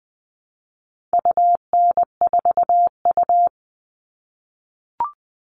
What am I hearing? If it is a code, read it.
Morse code, heard as a keyed tone, about 20 wpm, 700 Hz. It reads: UN4U
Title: All Cut Numbers - 4-Digit 20wpm